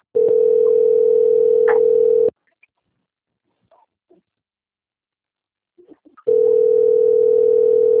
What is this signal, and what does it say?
Ringback tone (North America), 440 + 480 Hz.